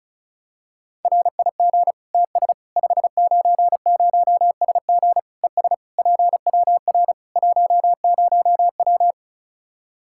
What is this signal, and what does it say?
Morse code, heard as a keyed tone, about 36 wpm, 700 Hz. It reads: RIG TS 590SG ES PWR 10W
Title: Practice QSO - Exchange 2 - Info - 35wpm